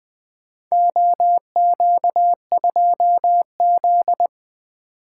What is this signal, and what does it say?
Morse code, heard as a keyed tone, about 20 wpm, 700 Hz. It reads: OQ2Z
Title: Four Letter-Number Combinations - 20wpm